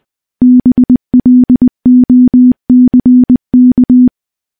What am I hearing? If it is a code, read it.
Morse code, heard as a keyed tone, about 20 wpm, 250 Hz. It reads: BLOCK
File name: morse / morse-training-20WPM